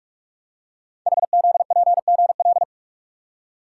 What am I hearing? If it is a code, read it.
Morse code, heard as a keyed tone, about 44 wpm, 700 Hz. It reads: HZPGL
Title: Code Groups - 5 Letters - 45wpm